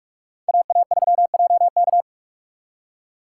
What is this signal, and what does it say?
Morse code, heard as a keyed tone, about 58 wpm, 700 Hz. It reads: AA3JK